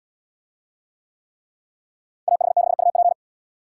Morse code, heard as a keyed tone, about 71 wpm, 700 Hz. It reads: AH6UP